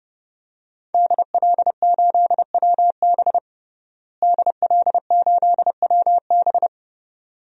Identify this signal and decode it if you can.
Morse code, heard as a keyed tone, about 30 wpm, 700 Hz. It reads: DL8WB DL8WB
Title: DXCC-100 Call Signs - 5 Characters - Repeated 2x - 30wpm